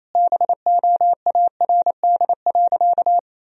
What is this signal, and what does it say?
Morse code, heard as a keyed tone, about 28 wpm, 700 Hz. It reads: BOARD.
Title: Sentences from Top 1000 Words - Review Part 4 - 28wpm